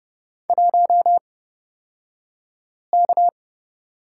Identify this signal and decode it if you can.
Morse code, heard as a keyed tone, about 31 wpm, spaced as for 5 wpm, 700 Hz. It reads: 1K
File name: Lesson 37d - Letter x - Two Characters 30wpm